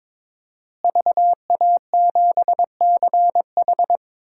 Morse code, heard as a keyed tone, about 22 wpm, 700 Hz. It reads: VA7CH
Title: DXCC-100 Call Signs - 5 Characters - Repeated 2x - 22wpm